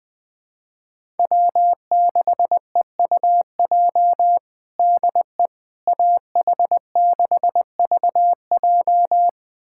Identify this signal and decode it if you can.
Morse code, heard as a keyed tone, about 20 wpm, 700 Hz. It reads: W6EUJ DE AH6VJ
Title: USA Callsign DE Callsign - 5 Characters - 20wpm